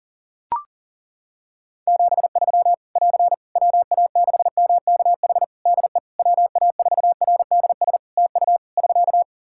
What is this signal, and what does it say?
Morse code, heard as a keyed tone, about 40 wpm, 700 Hz. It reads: 73 <AR> WA6MKH DE WA4RDS TU <SK>
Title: Practice QSO - Exchange 4 - Ending - 40wpm